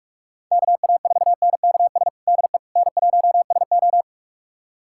Morse code, heard as a keyed tone, about 42 wpm, 700 Hz. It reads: KA4NKS DE N1SO
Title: USA Callsign DE Callsign 45wpm